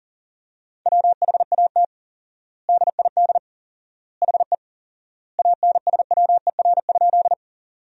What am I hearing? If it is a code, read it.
Morse code, heard as a keyed tone, about 39 wpm, 700 Hz. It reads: WHAT DID HE ANSWER?
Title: 4x Word-Spacing - Sentences from Top 300 Words - 40wpm